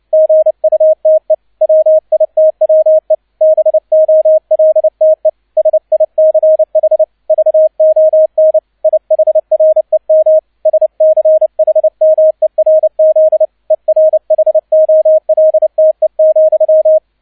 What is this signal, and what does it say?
Morse code, heard as a keyed tone, about 29 wpm, 630 Hz. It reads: GUTE WITWE BOLTE SICH VON IHREM SCHMERZ ERHOLTE,
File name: 002 Zweiter Streich 30wpm